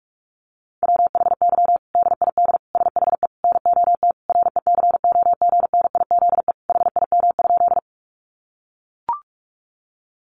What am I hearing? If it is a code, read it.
Morse code, heard as a keyed tone, about 44 wpm, 700 Hz. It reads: WHY DID SHE NOT RECOGNIZE HIM?
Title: Sentences from Top 1400 Words 45wpm